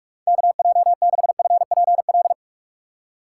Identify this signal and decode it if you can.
Morse code, heard as a keyed tone, about 44 wpm, 700 Hz. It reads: KJ6FPL
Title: USA Call Signs - 6 Characters - 6x Character Spacing - 45wpm